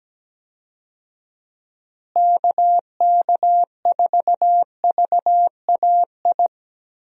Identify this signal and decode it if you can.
Morse code, heard as a keyed tone, about 17 wpm, 700 Hz. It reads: KK4VAI
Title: USA Call Signs - 6 Characters - Repeated 2x - 17wpm